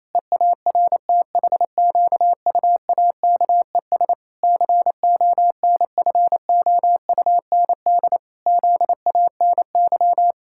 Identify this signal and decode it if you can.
Morse code, heard as a keyed tone, about 28 wpm, 700 Hz. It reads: EARTHQUAKES CONFOUND ZANY
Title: Pangram Challenge v2 - 28wpm